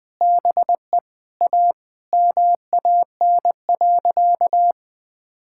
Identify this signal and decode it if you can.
Morse code, heard as a keyed tone, about 19 wpm, 700 Hz. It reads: BE A MAN.
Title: Sentences from Top 1600 Words 20wpm